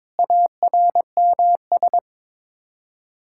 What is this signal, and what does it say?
Morse code, heard as a keyed tone, about 21 wpm, 700 Hz. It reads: ARMS
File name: Sets of 9 Words - 22wpm